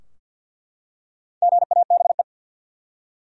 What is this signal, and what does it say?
Morse code, heard as a keyed tone, about 48 wpm, 700 Hz. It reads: GABE
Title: USA Common Names - Max 4 Letters - 50wpm